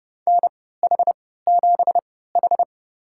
Morse code, heard as a keyed tone, about 29 wpm, spaced as for 17 wpm, 700 Hz. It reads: NH7H
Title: USA Call Signs - 4 Characters - 3x Character Spacing - 30wpm